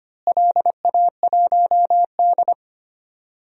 Morse code, heard as a keyed tone, about 25 wpm, 700 Hz. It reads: LA1D
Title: DXCC-100 Call Signs - 4 Characters - Repeated 3x - 25wpm